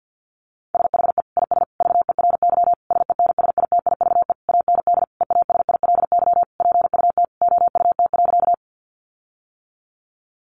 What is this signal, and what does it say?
Morse code, heard as a keyed tone, about 50 wpm, 700 Hz. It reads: SHE IS VERY SENSITIVE AND EASILY PUT OUT.
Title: Sentences from Top 1400 Words 50wpm